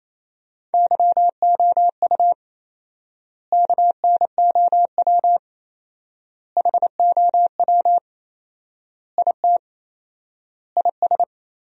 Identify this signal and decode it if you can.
Morse code, heard as a keyed tone, about 27 wpm, 700 Hz. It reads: YOU KNOW HOW IT IS
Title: 4x Word-Spacing - Sentences from Top 100 Words - 28wpm